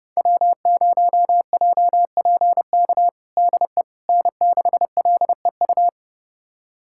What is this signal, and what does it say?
Morse code, heard as a keyed tone, about 30 wpm, 700 Hz. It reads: W0JPK DE N6LEU
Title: USA Callsign DE Callsign - 5 Characters - 30wpm